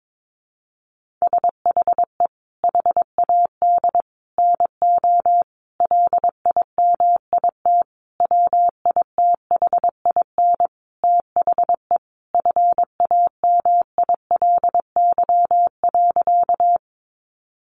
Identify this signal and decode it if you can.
Morse code, heard as a keyed tone, about 22 wpm, 700 Hz. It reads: SHE HAD NO LIMIT WITHIN THE FAMILY.
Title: Sentences from Top 1500 Words 22wpm